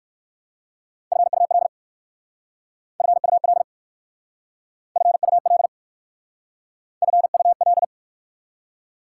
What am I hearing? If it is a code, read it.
Morse code, heard as a keyed tone, about 59 wpm, 690 Hz. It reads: FUL FUL FUL FUL